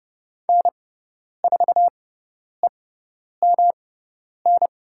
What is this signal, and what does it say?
Morse code, heard as a keyed tone, about 30 wpm, spaced as for 10 wpm, 700 Hz. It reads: N4EMN
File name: Lesson 18g - Letter m - Call Signs 30wpm